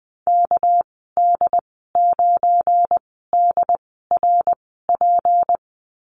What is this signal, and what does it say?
Morse code, heard as a keyed tone, about 20 wpm, spaced as for 14 wpm, 700 Hz. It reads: KD9DRP
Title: 245 Lesson 39g - Letter z - Call Signs 20wpm